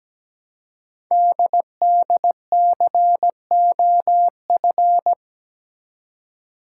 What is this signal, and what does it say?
Morse code, heard as a keyed tone, about 17 wpm, 700 Hz. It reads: DDCOF
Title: Code Groups - 5 Letters - 17wpm